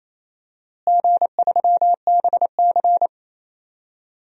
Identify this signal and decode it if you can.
Morse code, heard as a keyed tone, about 28 wpm, 700 Hz. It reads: G3BC
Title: DXCC-100 Call Signs - 4 Characters - Repeated 2x - 28wpm